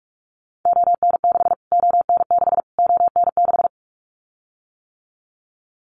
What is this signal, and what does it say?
Morse code, heard as a keyed tone, about 44 wpm, 700 Hz. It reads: ON6 ON6 ON6
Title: Callsign Prefixes for Top 25 DXCC - Repeated 3x - 45wpm